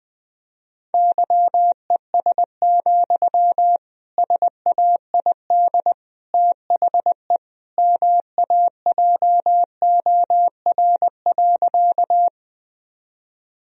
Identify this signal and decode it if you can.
Morse code, heard as a keyed tone, about 20 wpm, 700 Hz. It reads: YES, SAID THE MAJOR.
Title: Sentences from Top 1000 Words - Review Part 4 - 20wpm